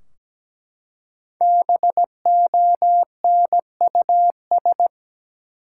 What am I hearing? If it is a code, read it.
Morse code, heard as a keyed tone, about 17 wpm, 700 Hz. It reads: BONUS
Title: Five Letter Words - 17wpm